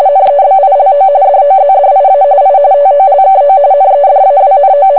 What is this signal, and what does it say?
RTTY, about 45.45 baud. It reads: NEBILO DOZDIA I TOLKO NEDAWNO